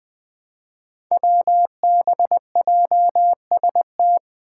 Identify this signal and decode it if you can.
Morse code, heard as a keyed tone, about 21 wpm, 700 Hz. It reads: WBJST
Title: Five Challenging Letter-Number Combinations 20wpm